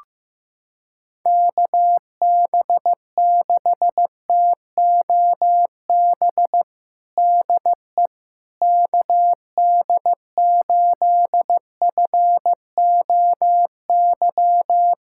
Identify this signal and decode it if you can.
Morse code, heard as a keyed tone, about 15 wpm, 700 Hz. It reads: KB6TOB DE KD8FOY
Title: USA Callsign DE Callsign 15wpm